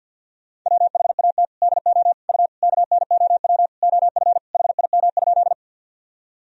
Morse code, heard as a keyed tone, about 52 wpm, 700 Hz. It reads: WHAT DO U KNOW OF HIM?
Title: Sentences from Top 100 Words and Abbreviations 50wpm